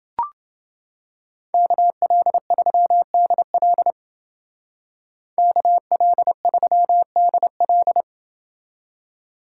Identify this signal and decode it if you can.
Morse code, heard as a keyed tone, about 29 wpm, 700 Hz. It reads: KL3DL KL3DL